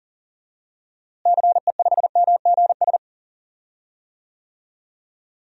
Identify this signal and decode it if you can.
Morse code, heard as a keyed tone, about 38 wpm, 700 Hz. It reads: CE5MGS
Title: DXCC-100 Call Signs - 6 Characters - Repeated 3x - 40wpm